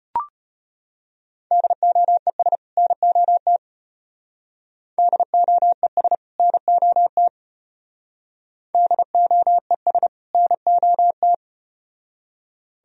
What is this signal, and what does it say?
Morse code, heard as a keyed tone, about 33 wpm, 700 Hz. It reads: DOES NOT DOES NOT DOES NOT